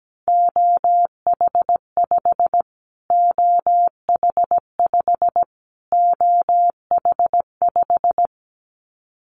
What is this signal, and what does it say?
Morse code, heard as a keyed tone, about 17 wpm, 700 Hz. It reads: OH5 OH5 OH5
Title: Callsign Prefixes for Top 25 DXCC - Repeated 3x - 17wpm